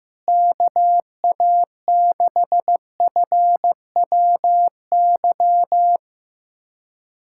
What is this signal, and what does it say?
Morse code, heard as a keyed tone, about 16 wpm, 700 Hz. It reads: KA6FWY
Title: USA Callsign DE Callsign - 6 Characters - 15wpm